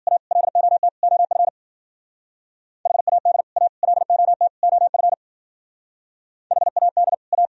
Morse code, heard as a keyed tone, about 51 wpm, 700 Hz. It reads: A LOT OF HAD A LOT OF HAD A